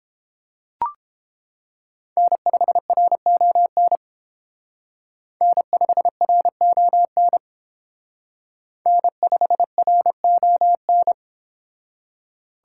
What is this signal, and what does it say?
Morse code, heard as a keyed tone, about 30 wpm, 700 Hz. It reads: N5RON N5RON N5RON